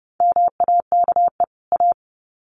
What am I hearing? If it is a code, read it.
Morse code, heard as a keyed tone, about 30 wpm, 700 Hz. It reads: MAKE A
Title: Sets of 2 Words 30wpm